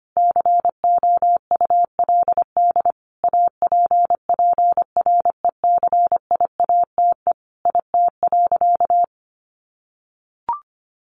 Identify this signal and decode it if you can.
Morse code, heard as a keyed tone, about 25 wpm, 700 Hz. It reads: COULD APPRECIATE IT.
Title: Sentences from Top 1300 Words 25wpm